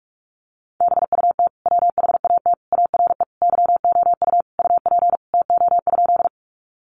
Morse code, heard as a keyed tone, about 44 wpm, 700 Hz. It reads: BUT WHAT ARE YOU UP TO?
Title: Sentences from Top 100 Words 45wpm